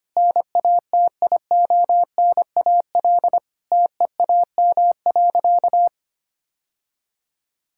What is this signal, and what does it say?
Morse code, heard as a keyed tone, about 25 wpm, 700 Hz. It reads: NATIONAL TEAM.
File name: practice / 25wpm / Sentences with 3-Word Phrases - 25wpm